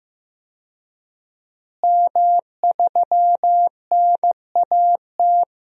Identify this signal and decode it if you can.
Morse code, heard as a keyed tone, about 15 wpm, 700 Hz. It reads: M3NAT
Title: DXCC-100 Call Signs - 5 Characters - Repeated 2x - 15wpm